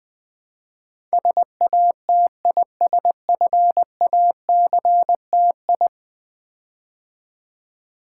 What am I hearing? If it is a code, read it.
Morse code, heard as a keyed tone, about 20 wpm, 700 Hz. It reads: SATISFACTI
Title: Build A Word - 5 Letters At A Time - 12-Letter Words - 20wpm